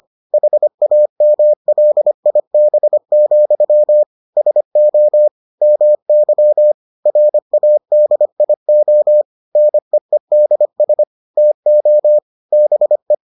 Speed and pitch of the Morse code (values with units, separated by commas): 25 wpm, 600 Hz